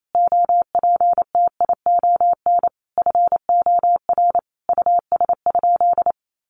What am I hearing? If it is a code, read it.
Morse code, heard as a keyed tone, about 28 wpm, 700 Hz. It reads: OPTION FOR US?
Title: Sentences from Top 1300 Words 28wpm